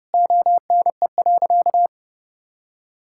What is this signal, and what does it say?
Morse code, heard as a keyed tone, about 30 wpm, 700 Hz. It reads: ONE.
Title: Sentences from Top 1000 Words - Review Part 5 - 30wpm